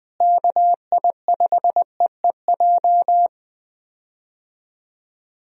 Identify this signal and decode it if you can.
Morse code, heard as a keyed tone, about 20 wpm, 700 Hz. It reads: KI5EEJ
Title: USA Call Signs - 6 Characters - Repeated 3x - 20wpm